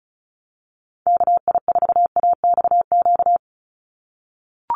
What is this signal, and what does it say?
Morse code, heard as a keyed tone, about 34 wpm, 700 Hz. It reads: KI4AXQ
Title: USA Call Signs 35wpm